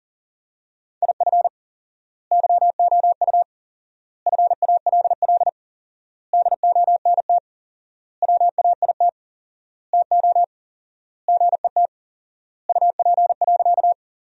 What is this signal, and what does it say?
Morse code, heard as a keyed tone, about 37 wpm, 700 Hz. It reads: IF YOU FALL DONT WAIT TO GET UP.
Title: 4x Word-Spacing - Sentences from Top 500 Words - 40wpm